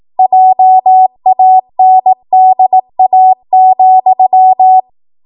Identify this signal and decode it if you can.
Morse code, heard as a keyed tone, about 18 wpm, 750 Hz. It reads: JANDA,
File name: Morse / 18WPM